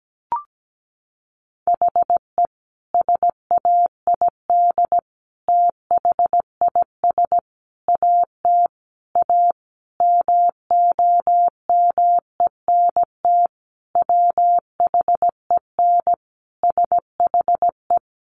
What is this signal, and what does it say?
Morse code, heard as a keyed tone, about 17 wpm, 700 Hz. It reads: HE SAID THIS AT A MOMENT WHEN SHE
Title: Sentences from Top 1000 Words 17wpm